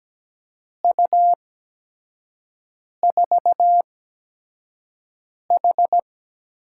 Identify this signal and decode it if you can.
Morse code, heard as a keyed tone, about 17 wpm, spaced as for 5 wpm, 700 Hz. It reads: U4H
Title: USA Call Signs - 4 Characters - 8x Character Spacing - 17wpm